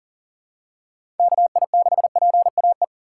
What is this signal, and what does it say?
Morse code, heard as a keyed tone, about 40 wpm, 700 Hz. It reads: KI6PAE
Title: USA Call Signs - 6 Characters - 4x Character Spacing - 40wpm